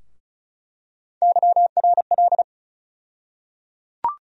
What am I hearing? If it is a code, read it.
Morse code, heard as a keyed tone, about 34 wpm, 700 Hz. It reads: YRL
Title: USA Call Sign Postfixes - 3 Letters 35wpm